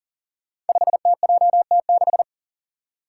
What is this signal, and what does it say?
Morse code, heard as a keyed tone, about 41 wpm, 700 Hz. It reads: 5TJT6
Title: Five Challenging Letter-Number Combinations 40wpm